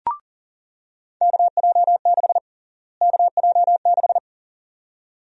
Morse code, heard as a keyed tone, about 39 wpm, 700 Hz. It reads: KJ6 KJ6
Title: USA Call Sign Prefixes - Repeated 2x - 40wpm